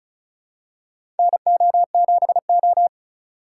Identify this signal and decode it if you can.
Morse code, heard as a keyed tone, about 36 wpm, 700 Hz. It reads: NO7O
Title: USA Call Signs - 4 Characters - 4x Character Spacing - 35wpm